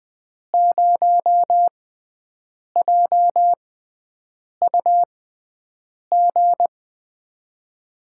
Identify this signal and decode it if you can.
Morse code, heard as a keyed tone, about 21 wpm, spaced as for 7 wpm, 700 Hz. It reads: 0JUG
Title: USA Call Signs - 6 Characters - 6x Character Spacing - 20wpm